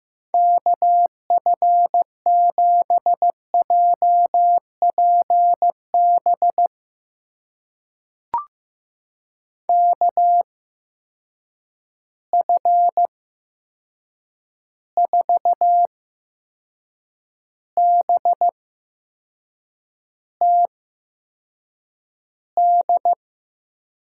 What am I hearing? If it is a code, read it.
Morse code, heard as a keyed tone, about 15 wpm, spaced as for 6 wpm, 700 Hz. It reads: KF7JPB KF4BTD
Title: USA Call Signs - 6 Characters - 8x Character Spacing - 15wpm